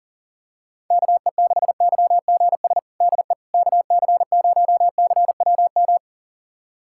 Morse code, heard as a keyed tone, about 40 wpm, 700 Hz. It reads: KE6YGS DE KC0CWM
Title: USA Callsign DE Callsign 40wpm